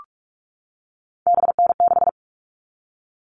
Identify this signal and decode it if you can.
Morse code, heard as a keyed tone, about 44 wpm, 700 Hz. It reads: BN6
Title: All Cut Numbers - 3-Digit 45wpm